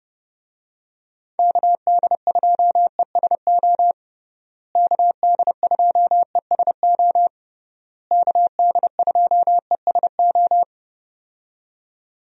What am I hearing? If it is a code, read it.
Morse code, heard as a keyed tone, about 32 wpm, 700 Hz. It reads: KD2ESO KD2ESO KD2ESO
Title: USA Call Signs - 6 Characters - Repeated 3x - 30wpm